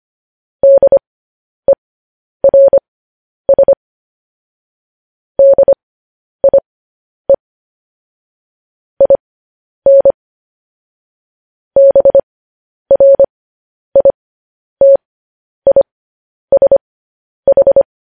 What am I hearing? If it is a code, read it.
Morse code, heard as a keyed tone, about 25 wpm, spaced as for 10 wpm, 550 Hz. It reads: DERS DIE IN BRITISH
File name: podcast / News headlines-1x-25x10